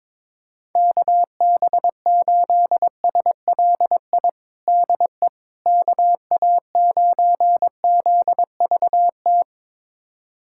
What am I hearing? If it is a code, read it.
Morse code, heard as a keyed tone, about 22 wpm, 700 Hz. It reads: KB8SLI DE KA9ZVT
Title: USA Callsign DE Callsign - 6 Characters - 22wpm